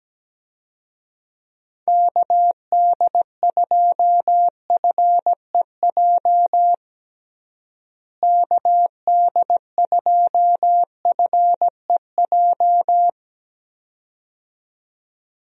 Morse code, heard as a keyed tone, about 17 wpm, 700 Hz. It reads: KD2FEJ KD2FEJ
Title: USA Call Signs - 6 Characters - Repeated 2x - 17wpm